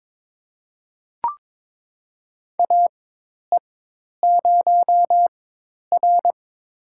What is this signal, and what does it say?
Morse code, heard as a keyed tone, about 22 wpm, spaced as for 10 wpm, 700 Hz. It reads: AE0R
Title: USA Call Signs - 4 Characters - 4x Character Spacing - 22wpm